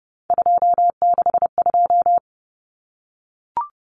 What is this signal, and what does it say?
Morse code, heard as a keyed tone, about 30 wpm, 700 Hz. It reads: 262
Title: Numbers - 3-Digit 30wpm